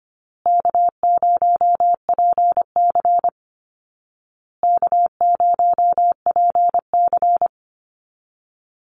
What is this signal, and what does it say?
Morse code, heard as a keyed tone, about 25 wpm, 700 Hz. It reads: K0PC K0PC
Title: Easy Call Signs Repeated 3x - 25wpm